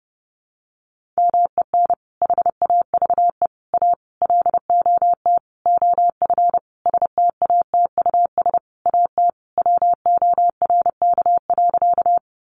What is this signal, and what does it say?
Morse code, heard as a keyed tone, about 30 wpm, 700 Hz. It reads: MEN HAVE A LOT OF STATUS AT WORK.
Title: Sentences from Top 1400 Words 30wpm